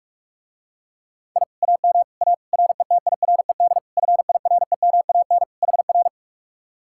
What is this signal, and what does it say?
Morse code, heard as a keyed tone, about 48 wpm, 700 Hz. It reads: I AM A RETIRED FIREMAN HR